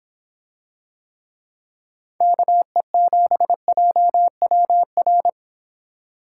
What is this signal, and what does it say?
Morse code, heard as a keyed tone, about 26 wpm, 700 Hz. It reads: KE7JWR